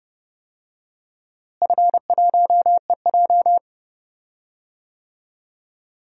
Morse code, heard as a keyed tone, about 32 wpm, 700 Hz. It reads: F1EJ
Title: DXCC-100 Call Signs - 4 Characters - Repeated 2x - 30wpm